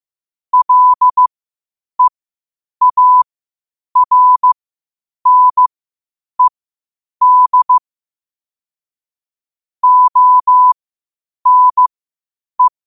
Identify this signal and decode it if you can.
Morse code, heard as a keyed tone, about 15 wpm, spaced as for 8 wpm, 1000 Hz. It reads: LEARNED ONE
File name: qotdfiles / 8wpm